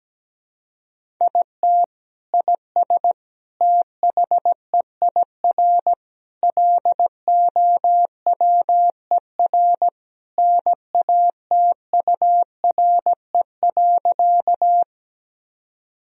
Morse code, heard as a keyed tone, about 17 wpm, 700 Hz. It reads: IT IS THEIR LOWER NATURE.
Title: Sentences from Top 1000 Words - Review Part 4 - 17wpm